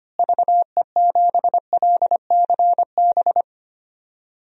Morse code, heard as a keyed tone, about 25 wpm, 700 Hz. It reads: VE7LCB
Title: DXCC-100 Call Signs - 6 Characters - Repeated 3x - 25wpm